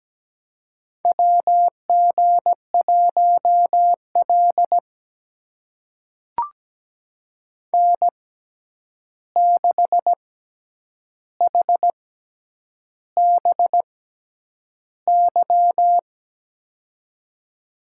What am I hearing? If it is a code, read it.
Morse code, heard as a keyed tone, about 17 wpm, spaced as for 8 wpm, 700 Hz. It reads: WG1L N6HBY
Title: USA Call Signs - 6x Character Spacing - 17wpm